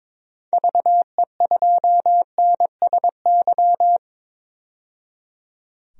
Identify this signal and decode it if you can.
Morse code, heard as a keyed tone, about 22 wpm, 700 Hz. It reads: VE2NSY
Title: DXCC-100 Call Signs - 6 Characters - 22wpm